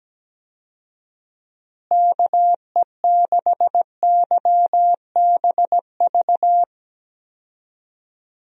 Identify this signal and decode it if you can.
Morse code, heard as a keyed tone, about 17 wpm, 700 Hz. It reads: KE6YBV
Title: USA Call Signs - 6 Characters - Repeated 3x - 17wpm